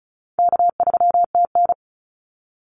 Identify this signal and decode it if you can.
Morse code, heard as a keyed tone, about 34 wpm, 700 Hz. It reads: K3TN
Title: Call Signs - 35wpm